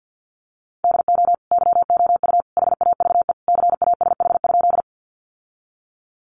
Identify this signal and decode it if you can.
Morse code, heard as a keyed tone, about 50 wpm, 700 Hz. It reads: DO YOU HAVE CASH?
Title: Sentences from Top 700 Words 50wpm